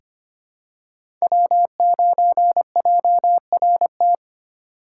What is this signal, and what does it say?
Morse code, heard as a keyed tone, about 25 wpm, 700 Hz. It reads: W9JRT
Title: USA Call Signs - 5 Characters - 3x Character Spacing - 25wpm